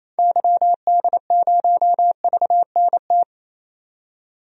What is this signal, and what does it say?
Morse code, heard as a keyed tone, about 28 wpm, 700 Hz. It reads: YD0VNT
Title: DXCC-100 Call Signs - 6 Characters - 28wpm